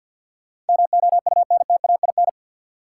Morse code, heard as a keyed tone, about 52 wpm, 700 Hz. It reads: MOUNTAIN
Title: Sentences from Top 1300 Words - New Word Drill Alt - 50wpm